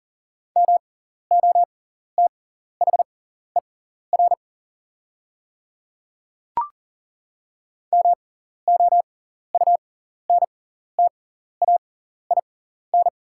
Morse code, heard as a keyed tone, about 38 wpm, spaced as for 14 wpm, 690 Hz. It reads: MOTHER MOUNTAIN
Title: 6x Character Spacing - Top 300 Words - 40wpm